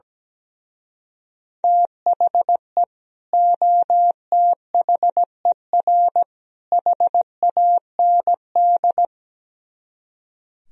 Morse code, heard as a keyed tone, about 18 wpm, 700 Hz. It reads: THE OTHER HAND
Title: Speed-Racing - Sets of 3 Words 17wpm